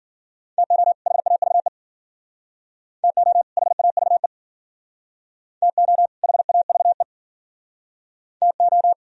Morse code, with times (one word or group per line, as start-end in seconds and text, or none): 0.6-0.9 TO
1.1-1.7 HAVE
3.0-3.4 TO
3.6-4.3 HAVE
5.6-6.1 TO
6.2-7.0 HAVE
8.4-8.9 TO